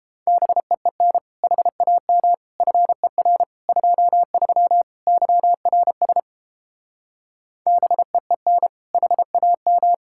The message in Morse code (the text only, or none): BEEN HAM FER 23 YRS BEEN HAM